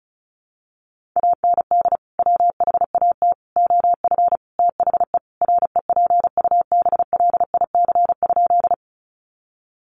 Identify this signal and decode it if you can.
Morse code, heard as a keyed tone, about 34 wpm, 700 Hz. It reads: AND WHAT OF THE REPUBLIC?
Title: Sentences from Top 1500 Words 35wpm